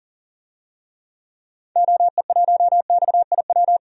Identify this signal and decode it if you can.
Morse code, heard as a keyed tone, about 40 wpm, 700 Hz. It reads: OE1XIW
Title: DXCC-100 Call Signs - 6 Characters - 40wpm